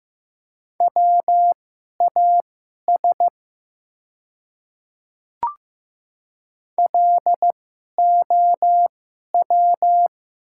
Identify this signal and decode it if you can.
Morse code, heard as a keyed tone, about 15 wpm, spaced as for 11 wpm, 700 Hz. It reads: WAS LOW
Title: 2x Character Spacing - Top 200 Words - 15wpm